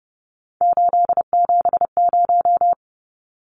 Morse code, heard as a keyed tone, about 30 wpm, 700 Hz. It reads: 870
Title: Numbers - 3-Digit 30wpm